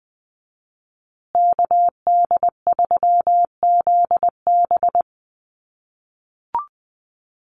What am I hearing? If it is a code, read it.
Morse code, heard as a keyed tone, about 20 wpm, 700 Hz. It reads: KD3ZB
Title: USA Call Signs Repeated 2x 20wpm